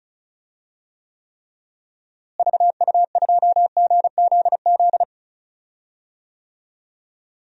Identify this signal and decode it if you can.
Morse code, heard as a keyed tone, about 36 wpm, 700 Hz. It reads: VU2GZZ
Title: DXCC-100 Call Signs - 6 Characters - Repeated 3x - 35wpm